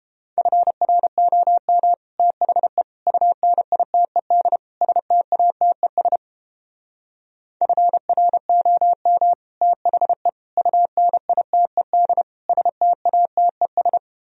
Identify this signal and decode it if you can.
Morse code, heard as a keyed tone, about 32 wpm, 700 Hz. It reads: FROM THE UNITED STATES FROM THE UNITED STATES